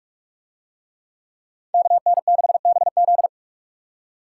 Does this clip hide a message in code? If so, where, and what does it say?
Yes: Morse code from 1.7 to 3.3 s, heard as a keyed tone, about 44 wpm, 700 Hz. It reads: KN6BZ